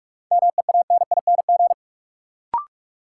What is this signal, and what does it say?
Morse code, heard as a keyed tone, about 47 wpm, 700 Hz. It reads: MEANING
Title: Sentences from Top 900 Words - New Word Drill - 45wpm